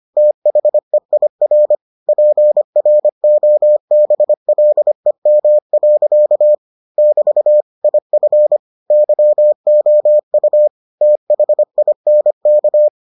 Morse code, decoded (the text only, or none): THEIR PROBLEM. <BT> IF YOU THINK